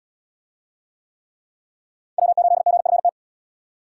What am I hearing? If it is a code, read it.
Morse code, heard as a keyed tone, about 76 wpm, 700 Hz. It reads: W8GFT